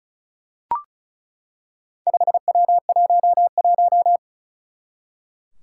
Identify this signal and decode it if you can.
Morse code, heard as a keyed tone, about 36 wpm, 700 Hz. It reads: 5W11
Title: Four Challenging Letter-Number Combinations 35wpm